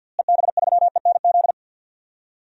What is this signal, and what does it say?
Morse code, heard as a keyed tone, about 50 wpm, 700 Hz. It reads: EB3ENZ
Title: DXCC-100 Call Signs - 6 Characters - Repeated 2x - 50wpm